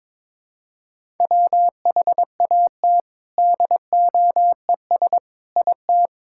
Morse code, heard as a keyed tone, about 22 wpm, 700 Hz. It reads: WHAT DOES IT
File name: Sentences with 7-Word Phrases - 22wpm